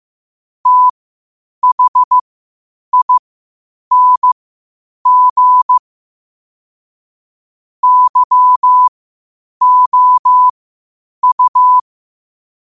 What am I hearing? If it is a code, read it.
Morse code, heard as a keyed tone, about 15 wpm, spaced as for 8 wpm, 1000 Hz. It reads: THING YOU
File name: qotdfiles / 8wpm